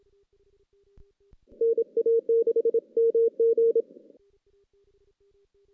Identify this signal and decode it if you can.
Morse code, heard as a keyed tone, about 27 wpm, 460 Hz. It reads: NA6MG
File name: 2016 JAN CW NAQP / 2016 Jan naqp k5ot - 0-6